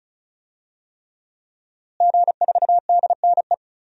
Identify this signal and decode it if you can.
Morse code, heard as a keyed tone, about 36 wpm, 700 Hz. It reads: G4DNE
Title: DXCC-100 Call Signs - 5 Characters - Repeated 3x - 35wpm